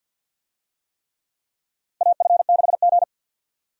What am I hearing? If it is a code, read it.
Morse code, heard as a keyed tone, about 47 wpm, 700 Hz. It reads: AF6G
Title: USA Call Signs - 4 Characters - Repeated 3x - 50wpm